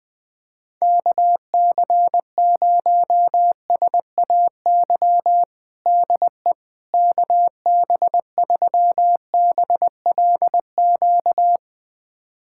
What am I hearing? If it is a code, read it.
Morse code, heard as a keyed tone, about 20 wpm, 700 Hz. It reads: KC0SAY DE KB3BLQ
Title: USA Callsign DE Callsign - 6 Characters - 20wpm